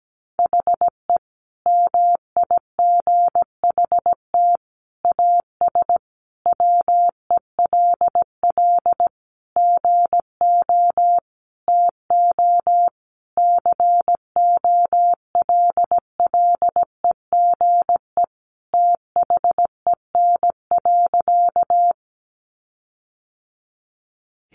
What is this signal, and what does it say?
Morse code, heard as a keyed tone, about 17 wpm, 700 Hz. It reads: HE MIGHT AS WELL GO TO COLLEGE THEN.
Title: Sentences from Top 1300 Words 17wpm